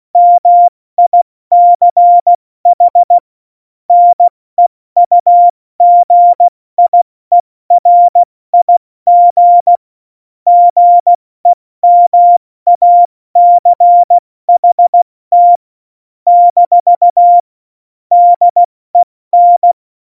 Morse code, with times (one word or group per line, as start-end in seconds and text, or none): 0.2-3.2 MICH
3.9-9.8 NEUGIERIG
10.5-15.6 GEMACHT
16.3-17.4 -
18.1-19.7 DEN